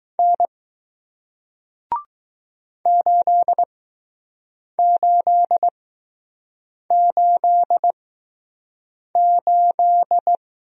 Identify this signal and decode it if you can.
Morse code, heard as a keyed tone, about 20 wpm, spaced as for 6 wpm, 700 Hz. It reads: N 8888